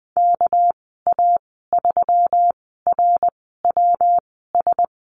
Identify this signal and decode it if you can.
Morse code, heard as a keyed tone, about 20 wpm, spaced as for 15 wpm, 700 Hz. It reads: KA3RWS
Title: 213 Lesson 34g - Number 8 - Call Signs 20wpm